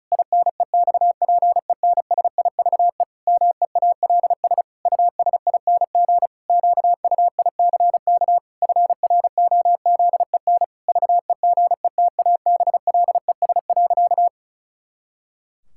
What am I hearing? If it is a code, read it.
Morse code, heard as a keyed tone, about 36 wpm, 700 Hz. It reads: INEXPENSIVE MEALS USING QUICK FROZEN VEGETABLES.
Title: Pangram Challenge v2 - 35wpm